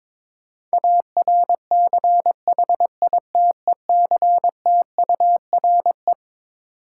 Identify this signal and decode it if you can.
Morse code, heard as a keyed tone, about 22 wpm, 700 Hz. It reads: ARCHITECTURE
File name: Twelve Letter Words - 22wpm